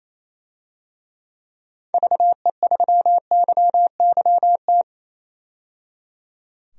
Morse code, heard as a keyed tone, about 28 wpm, 700 Hz. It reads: VE3YYT
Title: DXCC-100 Call Signs - 6 Characters - 28wpm